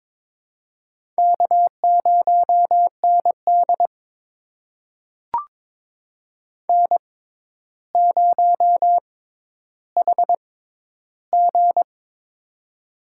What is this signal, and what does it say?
Morse code, heard as a keyed tone, about 22 wpm, spaced as for 11 wpm, 700 Hz. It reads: K0ND N0HG
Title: USA Call Signs - 4 Characters - 6x Character Spacing - 22wpm